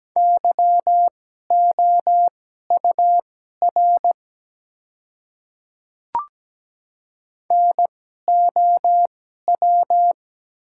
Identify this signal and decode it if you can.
Morse code, heard as a keyed tone, about 17 wpm, spaced as for 12 wpm, 700 Hz. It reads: YOUR NOW
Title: 2x Character Spacing - Top 300 Words - 17wpm